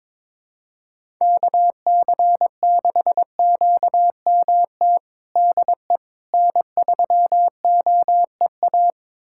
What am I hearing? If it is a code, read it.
Morse code, heard as a keyed tone, about 22 wpm, 700 Hz. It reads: KC6QMT DE N3OEA
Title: USA Callsign DE Callsign 22wpm